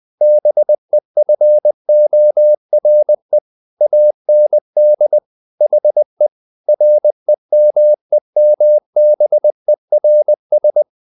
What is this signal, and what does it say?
Morse code, heard as a keyed tone, about 20 wpm, 600 Hz. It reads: BEFORE AND HE REMEMBERS
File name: the old man and the sea / the old man and the sea part 1 20wpm